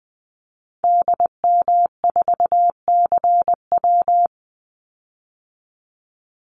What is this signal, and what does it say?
Morse code, heard as a keyed tone, about 20 wpm, 700 Hz. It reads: DM4CW
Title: Call Signs - 20wpm